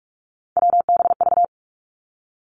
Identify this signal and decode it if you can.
Morse code, heard as a keyed tone, about 46 wpm, 700 Hz. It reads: WBV
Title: Three Challenging Letter-Number Combinations 45wpm